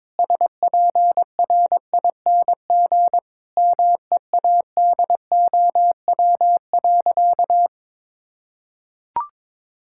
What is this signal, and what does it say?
Morse code, heard as a keyed tone, about 22 wpm, 700 Hz. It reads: SPRING MEADOW.
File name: Difficult Sentences with Hints - 22wpm